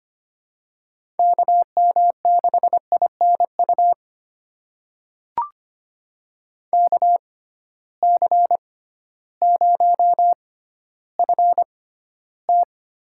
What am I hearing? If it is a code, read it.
Morse code, heard as a keyed tone, about 25 wpm, spaced as for 13 wpm, 700 Hz. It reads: KM6INU KC0FT
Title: USA Call Signs - 6 Characters - 6x Character Spacing - 25wpm